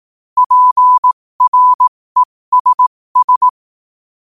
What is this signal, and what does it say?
Morse code, heard as a keyed tone, about 18 wpm, spaced as for 15 wpm, 1000 Hz. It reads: PRESS
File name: qotdfiles / 15wpm